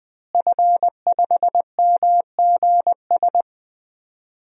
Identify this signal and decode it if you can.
Morse code, heard as a keyed tone, about 20 wpm, 700 Hz. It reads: F5MGS
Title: DXCC-100 Call Signs - 5 Characters - Repeated 3x - 20wpm